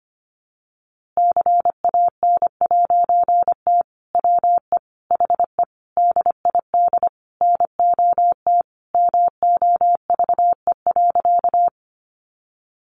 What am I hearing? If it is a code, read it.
Morse code, heard as a keyed tone, about 25 wpm, 700 Hz. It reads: CAN'T WE HE DID NOT MOVE.
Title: A Sound Of Thunder 25wpm